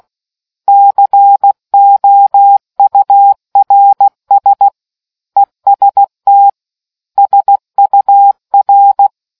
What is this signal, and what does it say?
Morse code, heard as a keyed tone, about 16 wpm, 800 Hz. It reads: COURS EST SUR